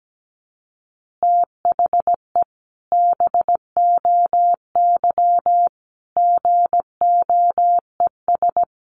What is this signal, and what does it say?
Morse code, heard as a keyed tone, about 17 wpm, 700 Hz. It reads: THE BOY GOES
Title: Additional Easy Phrases 17wpm